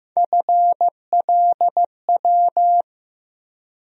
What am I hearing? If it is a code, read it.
Morse code, heard as a keyed tone, about 15 wpm, 700 Hz. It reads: FLW
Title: USA Call Sign Postfixes - 3 Letters 15wpm